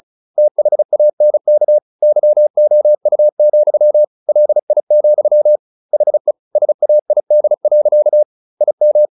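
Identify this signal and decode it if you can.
Morse code, heard as a keyed tone, about 34 wpm, 600 Hz. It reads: THANK YOU, LI, HE SAID. IM
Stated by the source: Morse practice at 35 wpm from CW med Gnister, sending Ken Follett's novel Never in English.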